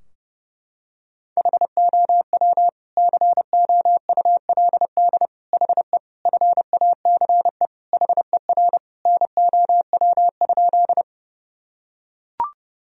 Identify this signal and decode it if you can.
Morse code, heard as a keyed tone, about 30 wpm, 700 Hz. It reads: HOW COULD HE FACE HER NOW?
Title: Sentences from Top 1000 Words - Review Part 2 - 30wpm